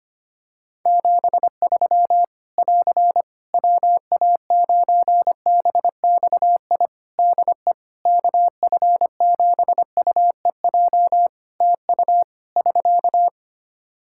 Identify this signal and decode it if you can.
Morse code, heard as a keyed tone, about 25 wpm, 700 Hz. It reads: 73 <AR> WA9BXI DE KF7UEJ TU <SK>
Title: K1USN SST QSO Practice 25wpm